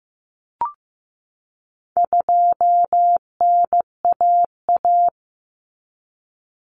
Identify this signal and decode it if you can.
Morse code, heard as a keyed tone, about 15 wpm, 700 Hz. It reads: 2NAA
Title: Common Cut Numbers - 4-Digit 15wpm